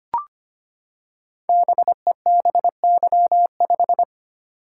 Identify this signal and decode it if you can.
Morse code, heard as a keyed tone, about 25 wpm, 700 Hz. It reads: BEBY5
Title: Code Groups - 5 Letter:Numbers - 25wpm